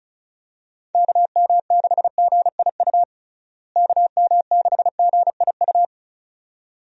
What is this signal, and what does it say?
Morse code, heard as a keyed tone, about 34 wpm, 700 Hz. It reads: KM6GIU KM6GIU
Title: USA Call Signs - 6 Characters - Repeated 2x - 35wpm